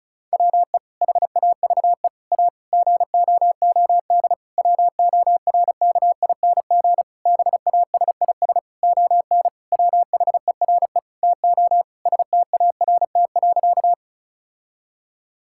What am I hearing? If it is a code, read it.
Morse code, heard as a keyed tone, about 36 wpm, 700 Hz. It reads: WE HAVE A GOOD WORKING BASIS ON WHERE TO START.
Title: Sentences from Top 1000 Words - Review Part 5 - 35wpm